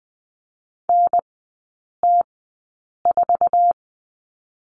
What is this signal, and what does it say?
Morse code, heard as a keyed tone, about 20 wpm, spaced as for 8 wpm, 700 Hz. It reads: NT4
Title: 047 Lesson 9e - Number 4 - Three Characters 20wpm